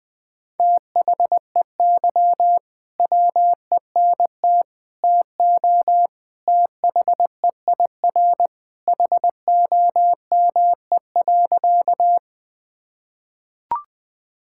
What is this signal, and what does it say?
Morse code, heard as a keyed tone, about 20 wpm, 700 Hz. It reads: THEY WENT TO THEIR HOME.
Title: Sentences with 2-Word Phrases - 20wpm